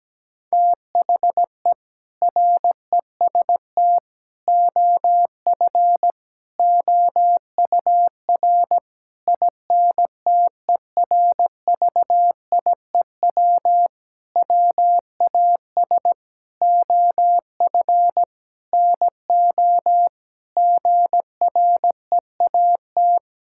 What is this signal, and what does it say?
Morse code, heard as a keyed tone, about 17 wpm, 700 Hz. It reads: THE REST OF OUR INTERVIEW WAS OF NO GREAT
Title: Sentences from Top 1600 Words 17wpm